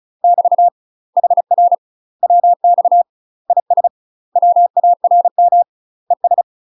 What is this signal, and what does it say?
Morse code, heard as a keyed tone, about 36 wpm, 700 Hz. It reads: <BT> HR WX IS WARM ES
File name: UK MASTER QSOs TRAINING / 35WPM / UK MASTER QSOs 0007